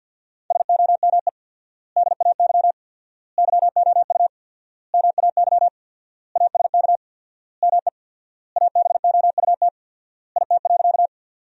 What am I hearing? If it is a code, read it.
Morse code, heard as a keyed tone, about 50 wpm, 700 Hz. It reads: SOME DAY YOU MAY ASK ME ABOUT IT.
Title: 4x Word-Spacing - Sentences from Top 200 Words - 50wpm